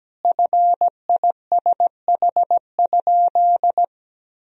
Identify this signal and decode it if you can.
Morse code, heard as a keyed tone, about 17 wpm, 700 Hz. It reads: FISH?
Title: Sentences from Top 1000 Words - Review Part 1 - 17wpm